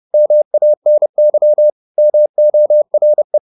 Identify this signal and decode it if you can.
Morse code, heard as a keyed tone, about 30 wpm, 600 Hz. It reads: MANY MORE
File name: the old man and the sea / the old man and the sea 1 30wpm